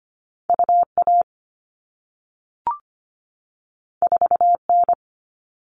Morse code, heard as a keyed tone, about 25 wpm, 700 Hz. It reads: U A 4 N
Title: All Cut Numbers - 2-Digit 25wpm